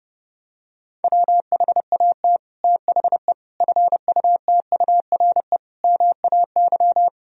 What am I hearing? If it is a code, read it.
Morse code, heard as a keyed tone, about 30 wpm, 700 Hz. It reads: WHAT THE FUTURE MAY
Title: Sentences from Top 1000 Words - Review Part 2 - 30wpm